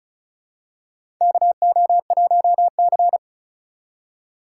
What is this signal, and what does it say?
Morse code, heard as a keyed tone, about 34 wpm, 700 Hz. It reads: KO1C
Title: USA Call Signs - 4 Characters - 2x Character Spacing - 35wpm